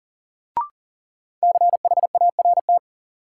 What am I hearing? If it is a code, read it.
Morse code, heard as a keyed tone, about 40 wpm, 700 Hz. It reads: CHART
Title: Sentences from Top 1300 Words - New Word Drill Alt - 40wpm